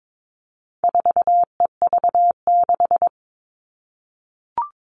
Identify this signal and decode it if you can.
Morse code, heard as a keyed tone, about 22 wpm, 700 Hz. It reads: 4EV6
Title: All Cut Numbers - 4-Digit 22wpm